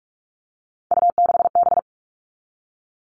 Morse code, heard as a keyed tone, about 44 wpm, 700 Hz. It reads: U6B
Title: All Cut Numbers - 3-Digit 45wpm